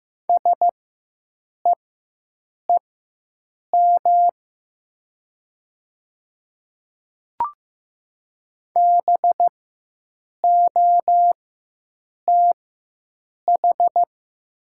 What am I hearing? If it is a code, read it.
Morse code, heard as a keyed tone, about 15 wpm, spaced as for 7 wpm, 700 Hz. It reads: SEEM BOTH
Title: 4x Character Spacing - Top 300 Words - 15wpm